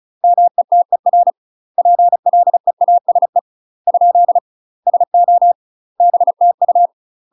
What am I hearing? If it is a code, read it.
Morse code, heard as a keyed tone, about 36 wpm, 700 Hz. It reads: METER PLEASE ? SO BTU
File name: UK MASTER QSOs TRAINING / 35WPM / UK MASTER QSOs 0017